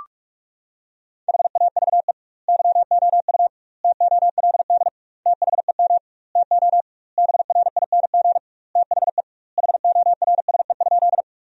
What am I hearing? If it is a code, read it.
Morse code, heard as a keyed tone, about 42 wpm, 700 Hz. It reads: HAVE YOU TOLD THEM TO BRING THE HORSE?
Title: 2x Word-Spacing - Sentences from Top 500 Words - 45wpm